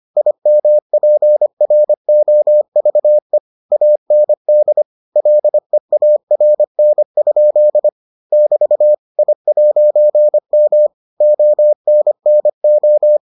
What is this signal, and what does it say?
Morse code, heard as a keyed tone, about 25 wpm, 600 Hz. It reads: IMPROVE AND LEARN? <BT> I'M ONNO